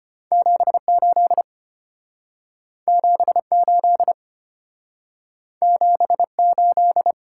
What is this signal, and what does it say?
Morse code, heard as a keyed tone, about 33 wpm, 690 Hz. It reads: 78 78 78